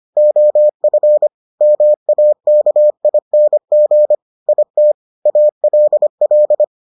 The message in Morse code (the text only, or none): OF MAKING IT ALL